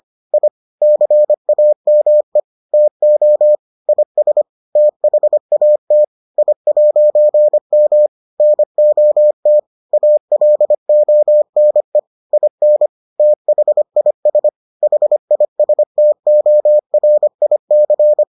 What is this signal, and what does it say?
Morse code, heard as a keyed tone, about 25 wpm, 600 Hz. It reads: I CAME TO IS THAT I'M NOT ALONE IN THIS HISTORIC